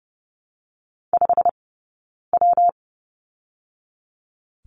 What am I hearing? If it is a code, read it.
Morse code, heard as a keyed tone, about 30 wpm, spaced as for 9 wpm, 700 Hz. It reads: 5W
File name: Lesson 20d - Number 3 - Two Characters 30wpm